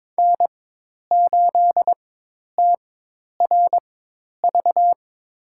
Morse code, heard as a keyed tone, about 22 wpm, spaced as for 10 wpm, 700 Hz. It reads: N8TRV
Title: USA Call Signs - 5 Characters - 4x Character Spacing - 22wpm